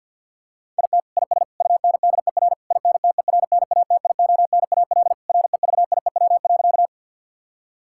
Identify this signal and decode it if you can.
Morse code, heard as a keyed tone, about 50 wpm, 700 Hz. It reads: IT IS UNDER INTERNATIONAL REVIEW.
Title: Sentences from Top 1000 Words - Review Part 6 - 50wpm